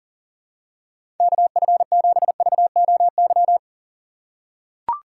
Morse code, heard as a keyed tone, about 38 wpm, 700 Hz. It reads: KF7VOY
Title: USA Call Signs - 6 Characters - 6x Character Spacing - 40wpm